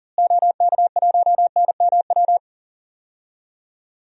Morse code, heard as a keyed tone, about 40 wpm, 700 Hz. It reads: OK1NMW
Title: DXCC-100 Call Signs - 6 Characters - Repeated 3x - 40wpm